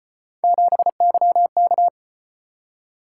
Morse code, heard as a keyed tone, about 33 wpm, 700 Hz. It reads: 7YK